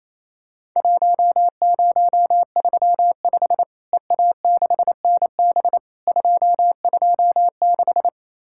Morse code, heard as a keyed tone, about 28 wpm, 700 Hz. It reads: 1035 EA6NB 226
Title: Worked All Europe DX Contest - QTC Exchange - 28wpm